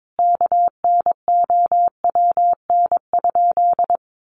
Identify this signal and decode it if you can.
Morse code, heard as a keyed tone, about 22 wpm, 700 Hz. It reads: KNOWN?
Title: Sentences from Top 1300 Words 22wpm